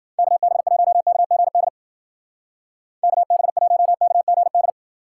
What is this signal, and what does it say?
Morse code, heard as a keyed tone, about 50 wpm, 700 Hz. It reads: KB1KGD KB1KGD
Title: Speed Racing - USA 2x3 Call Signs 40wpm